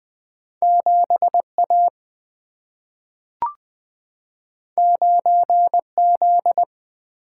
Morse code, heard as a keyed tone, about 20 wpm, 700 Hz. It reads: 7A 9Z
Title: Single-Letter Single-Number 20wpm